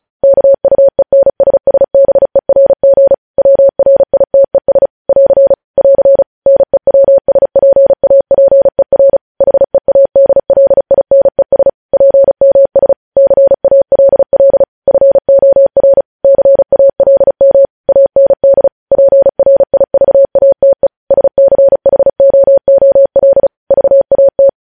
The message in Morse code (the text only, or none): KUENSSBERG WRITES <AR> <AR> NEWSPAPER HEADLINES PMS CALL FOR CALM AND PRIVATE SCHOOL VAT